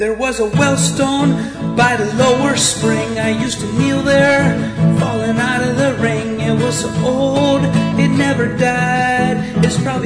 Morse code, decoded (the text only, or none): none